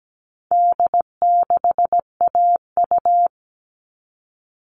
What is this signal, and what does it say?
Morse code, heard as a keyed tone, about 17 wpm, 700 Hz. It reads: D6AU
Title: All Cut Numbers - 4-Digit 17wpm